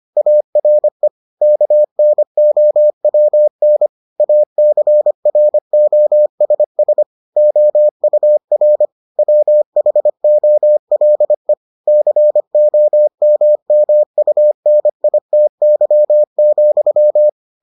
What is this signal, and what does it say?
Morse code, heard as a keyed tone, about 25 wpm, 600 Hz. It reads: ARE KNOWN ACROSS OUR WHOLE COMMUNITY,